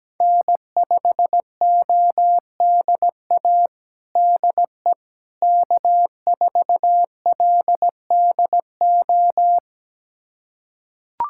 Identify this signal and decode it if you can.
Morse code, heard as a keyed tone, about 17 wpm, 700 Hz. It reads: N5ODA DE K4LDO
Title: USA Callsign DE Callsign - 5 Characters - 17wpm